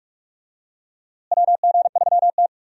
Morse code, heard as a keyed tone, about 46 wpm, 700 Hz. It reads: WG3T
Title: USA Call Signs - 4 Characters - Repeated 3x - 45wpm